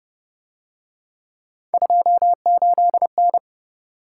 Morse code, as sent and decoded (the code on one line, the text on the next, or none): ..--- ---.. -.
28N